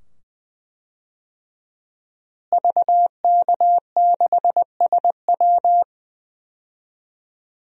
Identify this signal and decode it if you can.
Morse code, heard as a keyed tone, about 20 wpm, 700 Hz. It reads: VK6SW
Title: DXCC-100 Call Signs - 5 Characters - Repeated 2x - 20wpm